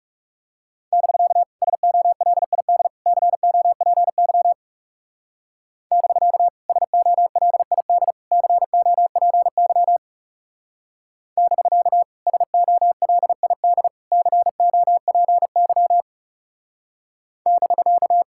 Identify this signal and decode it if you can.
Morse code, heard as a keyed tone, about 36 wpm, 700 Hz. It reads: <BK> SOLID COPY <BK> SOLID COPY <BK> SOLID COPY <BK>
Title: Speed-Racing Common QSO Phrases - 30wpm